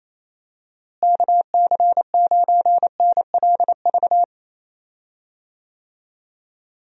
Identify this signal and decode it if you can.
Morse code, heard as a keyed tone, about 28 wpm, 700 Hz. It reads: KC9NLV
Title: USA Call Signs - 6 Characters - Repeated 3x - 28wpm